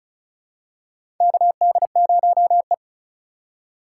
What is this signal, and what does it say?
Morse code, heard as a keyed tone, about 36 wpm, 700 Hz. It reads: KD0E
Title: USA Call Signs - 4 Characters - 3x Character Spacing - 35wpm